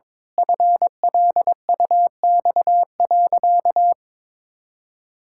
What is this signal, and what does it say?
Morse code, heard as a keyed tone, about 22 wpm, 700 Hz. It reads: FLUX.
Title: Pangram Challenge v2 - 22wpm